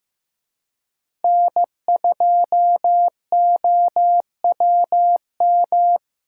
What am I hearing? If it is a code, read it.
Morse code, heard as a keyed tone, about 15 wpm, 700 Hz. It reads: N2OWM
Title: USA Call Signs - 5 Characters - Repeated 3x - 15wpm